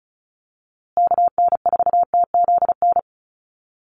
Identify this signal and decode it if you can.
Morse code, heard as a keyed tone, about 34 wpm, 700 Hz. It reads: KN4TZN
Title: USA Call Signs Repeated 2x 35wpm